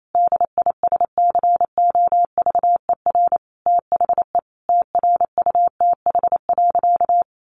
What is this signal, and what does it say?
Morse code, heard as a keyed tone, about 28 wpm, 700 Hz. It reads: DISCOVER THE TRUTH.
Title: Sentences from Top 1100 Words 28wpm